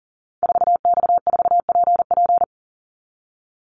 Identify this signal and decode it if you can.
Morse code, heard as a keyed tone, about 40 wpm, 700 Hz. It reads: 4X4PP
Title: Call Signs - 40wpm